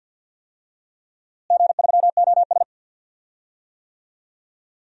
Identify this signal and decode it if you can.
Morse code, heard as a keyed tone, about 51 wpm, 700 Hz. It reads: G3OS
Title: DXCC-100 Call Signs - 4 Characters - Repeated 2x - 50wpm